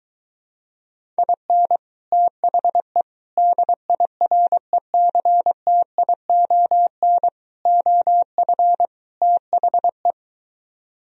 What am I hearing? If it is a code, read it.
Morse code, heard as a keyed tone, about 23 wpm, 700 Hz. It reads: IN THE DIRECTION OF THE